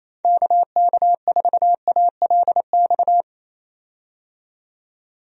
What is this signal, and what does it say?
Morse code, heard as a keyed tone, about 28 wpm, 700 Hz. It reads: KK4ALX
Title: USA Call Signs - 6 Characters - 28wpm